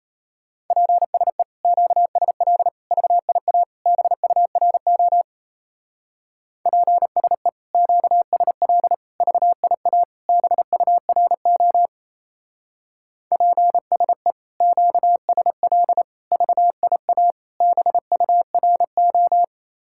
Morse code, heard as a keyed tone, about 35 wpm, 700 Hz. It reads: PSE QSL VIA BURO PSE QSL VIA BURO PSE QSL VIA BURO
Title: Speed-Racing Common QSO Phrases - 28wpm